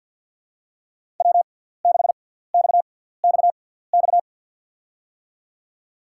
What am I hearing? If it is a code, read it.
Morse code, heard as a keyed tone, about 53 wpm, spaced as for 17 wpm, 700 Hz. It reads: W6XXX
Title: USA Call Signs - 5 Characters - 6x Character Spacing - 50wpm